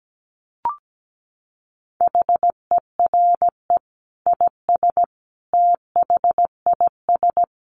Morse code, heard as a keyed tone, about 17 wpm, 700 Hz. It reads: HERE IS THIS
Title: Sentences from Top 1000 Words 17wpm